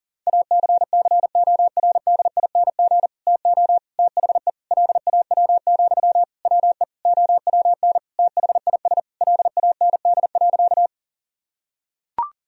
Morse code, heard as a keyed tone, about 40 wpm, 700 Hz. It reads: ACCORDING TO THE LAW, WE OWN THIS LAND.
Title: Sentences from Top 1000 Words - Review Part 4 - 40wpm